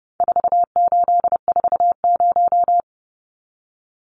Morse code, heard as a keyed tone, about 30 wpm, 700 Hz. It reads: 4840
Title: Numbers - 4-Digit 30wpm